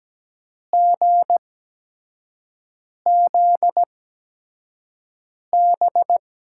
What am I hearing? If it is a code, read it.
Morse code, heard as a keyed tone, about 17 wpm, spaced as for 5 wpm, 700 Hz. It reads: GZB
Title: USA Call Signs - 6 Characters - 8x Character Spacing - 17wpm